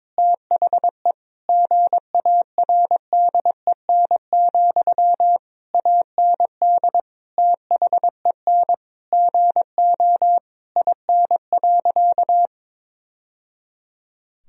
Morse code, heard as a keyed tone, about 22 wpm, 700 Hz. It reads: THE GARDEN, AND THEN GO IN.
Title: Sentences from Top 1000 Words - Review Part 4 - 22wpm